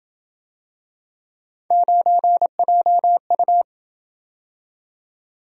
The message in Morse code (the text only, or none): 9JU